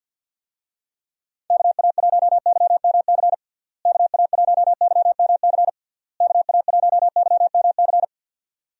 Morse code, heard as a keyed tone, about 50 wpm, 700 Hz. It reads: KA1YMC KA1YMC KA1YMC
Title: USA Call Signs - 6 Characters - Repeated 3x - 50wpm